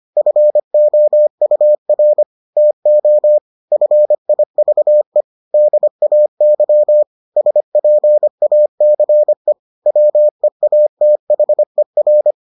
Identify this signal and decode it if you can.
Morse code, heard as a keyed tone, about 25 wpm, 600 Hz. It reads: FOUR TO FIVE DAY SPACE WEATHER